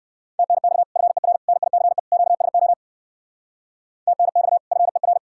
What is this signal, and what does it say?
Morse code, heard as a keyed tone, about 58 wpm, 700 Hz. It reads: TNX FER NICE QSO TNX FER